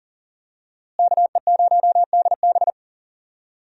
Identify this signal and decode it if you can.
Morse code, heard as a keyed tone, about 37 wpm, 700 Hz. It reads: KE0DB
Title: USA Call Signs - 5 Characters - Repeated 3x - 40wpm